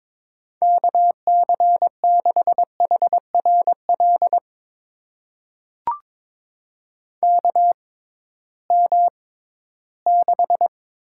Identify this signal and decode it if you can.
Morse code, heard as a keyed tone, about 22 wpm, spaced as for 14 wpm, 700 Hz. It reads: KC6HRL KM6
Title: USA Call Signs - 6 Characters - 6x Character Spacing - 22wpm